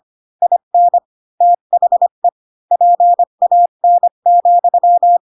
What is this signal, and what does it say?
Morse code, heard as a keyed tone, about 25 wpm, 700 Hz. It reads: IN THE PAN,
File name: Pinocchio / Pinocchio ch5 20 25 ik1hge